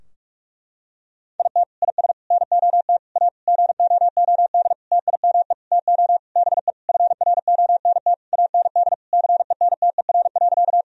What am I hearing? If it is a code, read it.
Morse code, heard as a keyed tone, about 46 wpm, 700 Hz. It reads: IT IS NOT A GOOD TIME TO BE FRONT AND CENTER.
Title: Sentences from Top 1000 Words - Review Part 2 - 45wpm